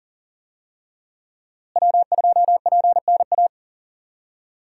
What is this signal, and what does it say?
Morse code, heard as a keyed tone, about 37 wpm, 700 Hz. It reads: W2PNA
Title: USA Call Signs - 5 Characters - Repeated 2x - 40wpm